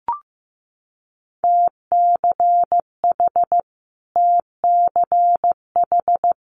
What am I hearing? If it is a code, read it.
Morse code, heard as a keyed tone, about 15 wpm, 700 Hz. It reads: TCH TCH
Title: Multi-letter Phonemes Repeated 3x 15wpm